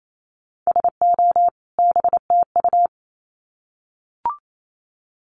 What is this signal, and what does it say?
Morse code, heard as a keyed tone, about 28 wpm, 700 Hz. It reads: SO BTU
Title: Common QSO Phrases 28wpm